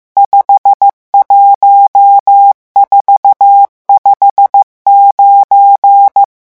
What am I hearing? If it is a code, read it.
Morse code, heard as a keyed tone, about 15 wpm, 800 Hz. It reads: 51459
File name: morse mp3 / revision11 chiffres vitesse 14